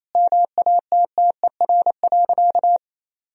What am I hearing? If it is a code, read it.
Morse code, heard as a keyed tone, about 28 wpm, 700 Hz. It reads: MATTER.
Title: Sentences from Top 1000 Words - Review Part 2 - 28wpm